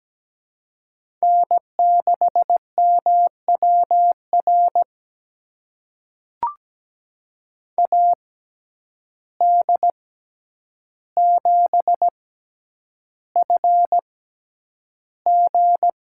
Morse code, heard as a keyed tone, about 17 wpm, spaced as for 9 wpm, 700 Hz. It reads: N6MWR AD7FG
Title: USA Call Signs - 5 Characters - 6x Character Spacing - 17wpm